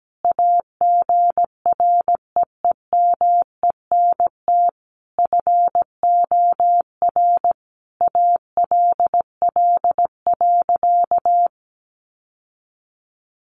Morse code, heard as a keyed tone, about 17 wpm, 700 Hz. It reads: AGREEMENT FOR ALL.
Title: Sentences from Top 1500 Words 17wpm